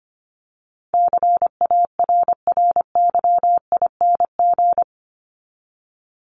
Morse code, heard as a keyed tone, about 25 wpm, 700 Hz. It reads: CARRYING
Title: ING Suffix - Top 100 Words with Base Words 25wpm